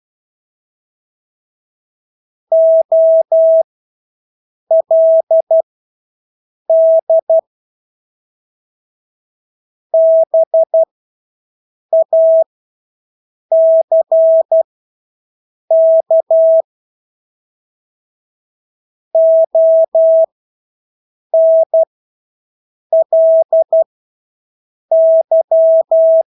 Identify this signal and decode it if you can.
Morse code, heard as a keyed tone, about 12 wpm, spaced as for 6 wpm, 650 Hz. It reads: OLD BACK ONLY
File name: Engelske ord 12-6 wpm